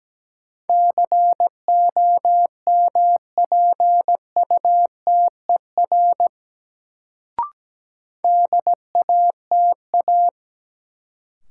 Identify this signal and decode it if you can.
Morse code, heard as a keyed tone, about 17 wpm, 700 Hz. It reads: COMPUTER DATA
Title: Sentences from Top 1000 Words - Review Part 3 - New Word Drill - 17wpm